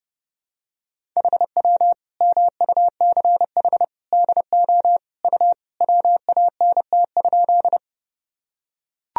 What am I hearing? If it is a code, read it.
Morse code, heard as a keyed tone, about 30 wpm, 700 Hz. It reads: HW MUCH DO U WANT?
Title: Sentences from Top 200 Words and Abbreviations 30wpm